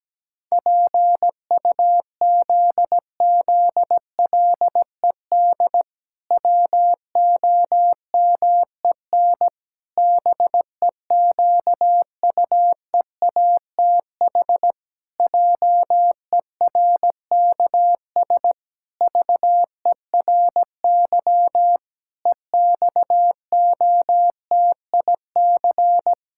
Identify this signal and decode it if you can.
Morse code, heard as a keyed tone, about 17 wpm, 700 Hz. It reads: PUZZLED WOMEN BEQUEATH JERKS VERY EXOTIC
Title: Pangram Challenge v2 - 17wpm